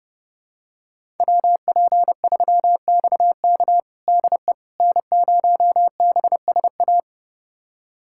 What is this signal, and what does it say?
Morse code, heard as a keyed tone, about 30 wpm, 700 Hz. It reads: WP3XK DE N0BSA
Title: USA Callsign DE Callsign - 5 Characters - 30wpm